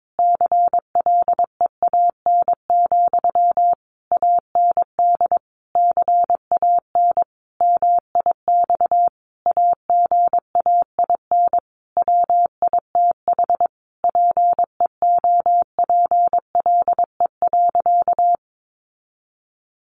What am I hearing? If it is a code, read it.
Morse code, heard as a keyed tone, about 22 wpm, 700 Hz. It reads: CLEAN, AND CAN MIX AGAIN WITH PEOPLE.
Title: Sentences from Top 1100 Words 22wpm